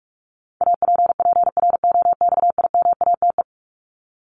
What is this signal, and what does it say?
Morse code, heard as a keyed tone, about 44 wpm, 700 Hz. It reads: APPROXIMATE
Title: LY Suffix - Top 100 Words with Base Words 45wpm